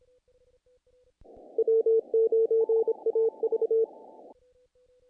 Morse code, heard as a keyed tone, about 26 wpm, 450 Hz. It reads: W9AV
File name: contests / naqp-cw-k5tr-op-k5pi- -1800